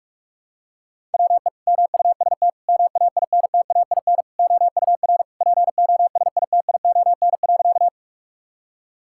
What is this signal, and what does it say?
Morse code, heard as a keyed tone, about 46 wpm, 700 Hz. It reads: WE MUST MAINTAIN OUR POSITION.
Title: Sentences from Top 1000 Words - Review Part 5 - 45wpm